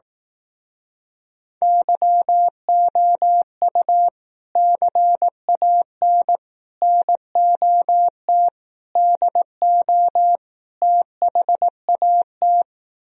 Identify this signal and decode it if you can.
Morse code, heard as a keyed tone, about 18 wpm, 700 Hz. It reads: YOU CAN NOT DO THAT
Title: Speed-Racing - Sets of 5 Words 15wpm